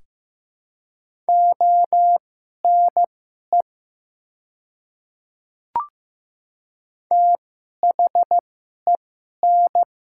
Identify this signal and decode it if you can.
Morse code, heard as a keyed tone, about 15 wpm, spaced as for 11 wpm, 700 Hz. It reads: ONE THEN
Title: 2x Character Spacing - Top 100 Words - 15wpm